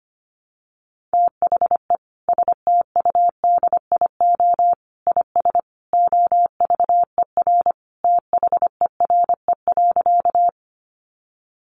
Morse code, heard as a keyed tone, about 25 wpm, 700 Hz. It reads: THE STUDIO IS OVER THERE.
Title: Sentences from Top 1200 Words 25wpm